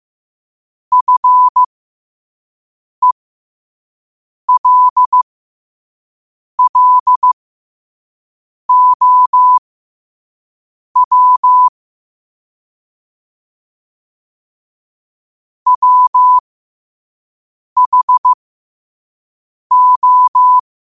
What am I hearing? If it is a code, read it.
Morse code, heard as a keyed tone, about 15 wpm, spaced as for 5 wpm, 1000 Hz. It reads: FELLOW WHO
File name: qotdfiles / 5wpm